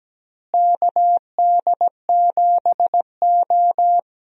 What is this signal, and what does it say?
Morse code, heard as a keyed tone, about 17 wpm, 700 Hz. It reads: KD7O
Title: USA Call Signs - 4 Characters - Repeated 3x - 17wpm